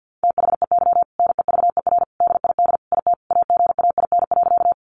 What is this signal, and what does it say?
Morse code, heard as a keyed tone, about 48 wpm, 700 Hz. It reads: THEY NEVER DID IT AGAIN.
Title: Sentences from Top 300 Words 50wpm